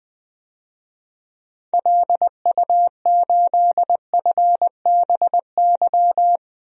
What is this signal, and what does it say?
Morse code, heard as a keyed tone, about 20 wpm, 700 Hz. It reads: LU8FBY
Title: DXCC-100 Call Signs - 6 Characters - 20wpm